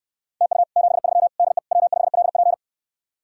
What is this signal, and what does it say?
Morse code, heard as a keyed tone, about 69 wpm, 700 Hz. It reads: TU 73 DE W5LP